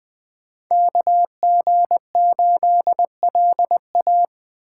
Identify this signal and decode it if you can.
Morse code, heard as a keyed tone, about 20 wpm, 700 Hz. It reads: KG8LA
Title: Speed Racing - USA Call Signs 17wpm